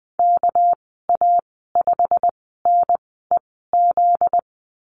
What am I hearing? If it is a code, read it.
Morse code, heard as a keyed tone, about 20 wpm, spaced as for 15 wpm, 700 Hz. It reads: KA5NEZ
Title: 245 Lesson 39g - Letter z - Call Signs 20wpm